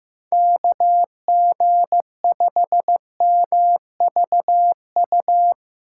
Morse code, heard as a keyed tone, about 15 wpm, 700 Hz. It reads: KG5MVU
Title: USA Call Signs - 6 Characters - 4x Character Spacing - 15wpm